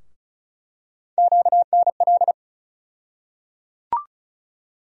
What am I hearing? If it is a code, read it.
Morse code, heard as a keyed tone, about 36 wpm, 700 Hz. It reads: QNL
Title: USA Call Sign Postfixes - 3 Letters 35wpm